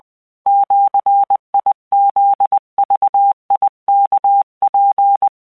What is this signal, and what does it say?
Morse code (Japanese Wabun code), heard as a keyed tone, about 20 wpm, 800 Hz. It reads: ジフグワツ